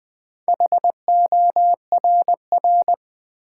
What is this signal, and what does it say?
Morse code, heard as a keyed tone, about 20 wpm, 700 Hz. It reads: HORR
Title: Build A Word - 2 Letters At A Time - 8-Letter Words - 20wpm